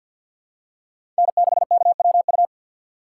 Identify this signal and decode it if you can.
Morse code, heard as a keyed tone, about 48 wpm, 700 Hz. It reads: N6KWU
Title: USA Call Signs - 5 Characters - 3x Character Spacing - 50wpm